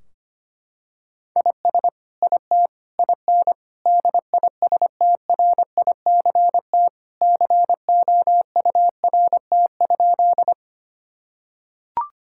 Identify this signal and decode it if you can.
Morse code, heard as a keyed tone, about 25 wpm, 700 Hz. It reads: IS IT IN DISTRICT COURT?
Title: Sentences from Top 1600 Words 25wpm